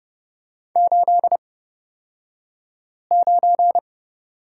Morse code, heard as a keyed tone, about 30 wpm, spaced as for 5 wpm, 700 Hz. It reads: 89
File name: Lesson 37d - Letter x - Two Characters 30wpm